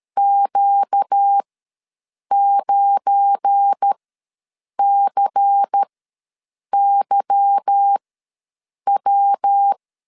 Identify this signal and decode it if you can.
Morse code, heard as a keyed tone, about 13 wpm, spaced as for 7 wpm, 800 Hz. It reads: Q9CYW
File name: LEZIONI / 9 2 ws 12wpm 3406